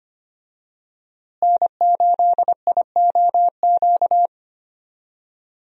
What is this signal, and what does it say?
Morse code, heard as a keyed tone, about 25 wpm, 700 Hz. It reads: N8IOQ
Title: USA Call Signs - 5 Characters - Repeated 2x - 25wpm